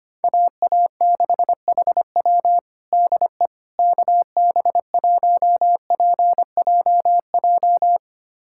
Morse code, heard as a keyed tone, about 25 wpm, 700 Hz. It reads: AA6HW DE KB1PJJ
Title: USA Callsign DE Callsign 25wpm